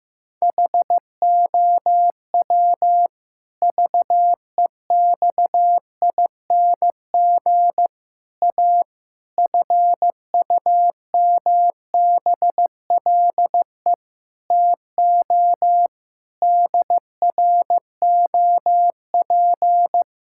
Morse code, heard as a keyed tone, about 15 wpm, 700 Hz. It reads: HOW VEXING A FUMBLE TO DROP
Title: Pangram Challenge v2 - 15wpm